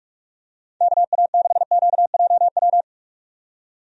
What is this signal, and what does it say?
Morse code, heard as a keyed tone, about 46 wpm, 700 Hz. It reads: KA6QJW
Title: USA Call Signs - 6 Characters - 45wpm